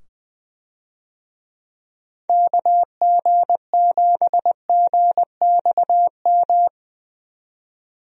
Morse code, heard as a keyed tone, about 20 wpm, 700 Hz. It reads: KG7GXM
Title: USA Call Signs - 6 Characters - 20wpm